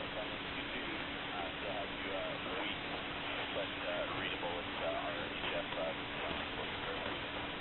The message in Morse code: none

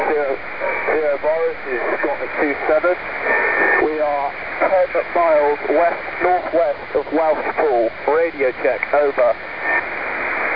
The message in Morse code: none

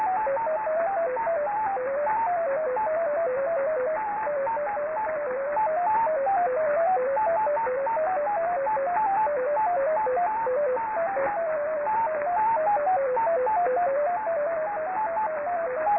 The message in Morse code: none

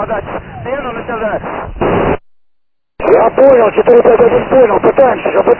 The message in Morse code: none

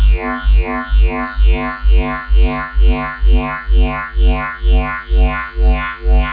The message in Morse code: none

